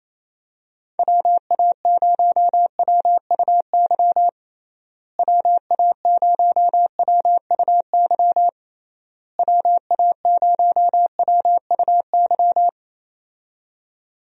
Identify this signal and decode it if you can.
Morse code, heard as a keyed tone, about 28 wpm, 700 Hz. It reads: WA0WUY WA0WUY WA0WUY
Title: USA Call Signs - 6 Characters - Repeated 3x - 28wpm